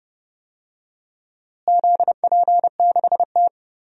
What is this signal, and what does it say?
Morse code, heard as a keyed tone, about 30 wpm, 700 Hz. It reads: ZP6T
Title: DXCC-100 Call Signs - 4 Characters - Repeated 2x - 30wpm